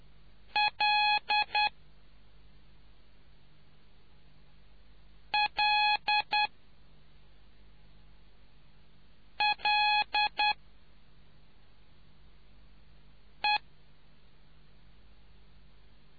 Morse code, heard as a keyed tone, about 10 wpm, spaced as for 3 wpm, 840 Hz. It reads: LLLE